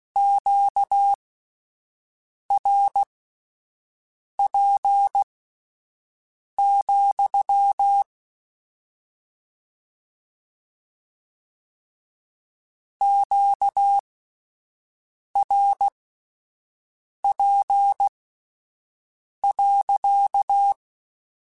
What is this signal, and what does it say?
Morse code, heard as a keyed tone, about 16 wpm, spaced as for 5 wpm, 800 Hz. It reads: QRP, QRP.
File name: Q Calls / Q Calls-mp3s / Q Calls sppc @10